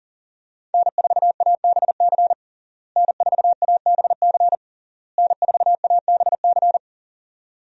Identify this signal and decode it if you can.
Morse code, heard as a keyed tone, about 37 wpm, 700 Hz. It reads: N4ABC N4ABC N4ABC
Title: USA Call Signs - 5 Characters - Repeated 3x - 40wpm